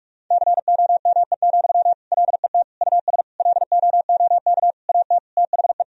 Morse code, heard as a keyed tone, about 46 wpm, 700 Hz. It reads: COME, LET US LOOK AT THE
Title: Sentences from Top 1000 Words - Review Part 3 - 45wpm